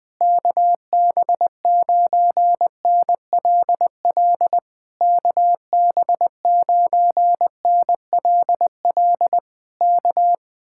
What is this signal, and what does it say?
Morse code, heard as a keyed tone, about 20 wpm, 700 Hz. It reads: KB9NLL KB9NLL K
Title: Speed-Racing Common QSO Phrases - 17wpm